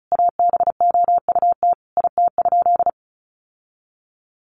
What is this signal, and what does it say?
Morse code, heard as a keyed tone, about 34 wpm, 700 Hz. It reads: ABOUT IT?
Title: Sentences from Top 1100 Words 35wpm